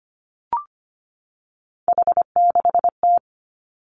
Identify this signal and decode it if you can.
Morse code, heard as a keyed tone, about 25 wpm, 700 Hz. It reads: H6T
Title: Three Challenging Letter-Number Combinations 25wpm